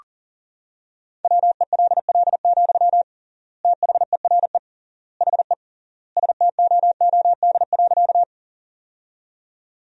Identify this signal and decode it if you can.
Morse code, heard as a keyed tone, about 40 wpm, 700 Hz. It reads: WELL, THERE HE STOOD.
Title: 3x Word-Spacing - Sentences from Top 500 Words - 40wpm